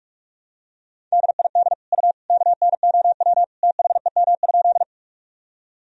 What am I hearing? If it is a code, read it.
Morse code, heard as a keyed tone, about 46 wpm, 700 Hz. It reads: DID U KNOW THEM?
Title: Sentences from Top 100 Words and Abbreviations 45wpm